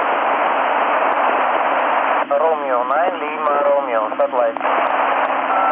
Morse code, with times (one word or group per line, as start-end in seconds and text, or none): none